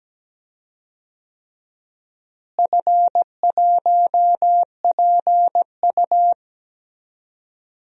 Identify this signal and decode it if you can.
Morse code, heard as a keyed tone, about 17 wpm, 700 Hz. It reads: F1PU
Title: DXCC-100 Call Signs - 4 Characters - Repeated 3x - 17wpm